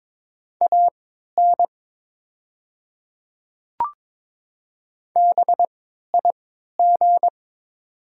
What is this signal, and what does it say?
Morse code, heard as a keyed tone, about 22 wpm, spaced as for 12 wpm, 700 Hz. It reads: AN BIG
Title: 3x Character Spacing - Top 200 Words - 22wpm